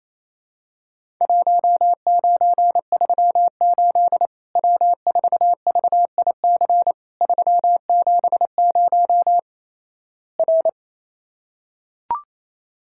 Morse code, heard as a keyed tone, about 28 wpm, 700 Hz. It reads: 1938 W4VIC 370 R
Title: Worked All Europe DX Contest - QTC Exchange - 28wpm